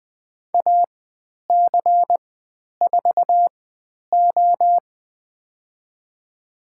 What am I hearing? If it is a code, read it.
Morse code, heard as a keyed tone, about 20 wpm, spaced as for 10 wpm, 700 Hz. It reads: AC4O
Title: 102 Lesson 17g - Letter u - Call Signs 20wpm